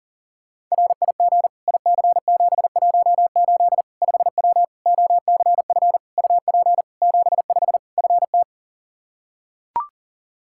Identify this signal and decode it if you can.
Morse code, heard as a keyed tone, about 40 wpm, 700 Hz. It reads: RIG IC718 5W OCF UP 75 FT
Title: K1USN SST QSO Practice 40wpm